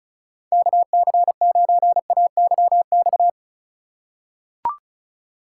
Morse code, heard as a keyed tone, about 36 wpm, 690 Hz. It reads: KC9AYX
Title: USA Call Signs - 6 Characters - 4x Character Spacing - 35wpm